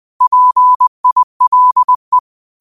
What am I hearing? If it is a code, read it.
Morse code, heard as a keyed tone, about 20 wpm, 1000 Hz. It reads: PILE
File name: qotdfiles / 20wpm